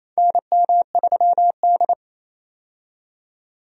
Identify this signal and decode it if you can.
Morse code, heard as a keyed tone, about 27 wpm, 700 Hz. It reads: NM3D
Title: USA Call Signs - 4 Characters - 28wpm